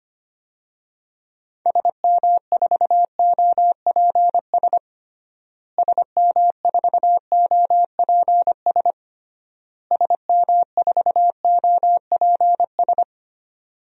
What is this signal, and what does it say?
Morse code, heard as a keyed tone, about 25 wpm, 700 Hz. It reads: SM4OPS SM4OPS SM4OPS
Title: DXCC-100 Call Signs - 6 Characters - Repeated 3x - 25wpm